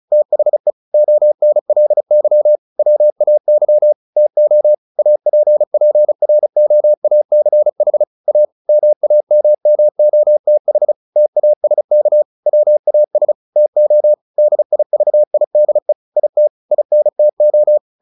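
Morse code, heard as a keyed tone, about 34 wpm, 600 Hz. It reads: THE ONLY WAY TO APPROACH A MAMMOTH TASK WAS TO DIVIDE IT INTO
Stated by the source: Morse practice at 35 wpm from CW med Gnister, sending Ken Follett's novel Never in English.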